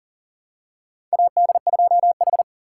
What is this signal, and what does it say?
Morse code, heard as a keyed tone, about 40 wpm, 700 Hz. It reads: AD2H
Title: USA Call Signs - 4 Characters - 6x Character Spacing - 40wpm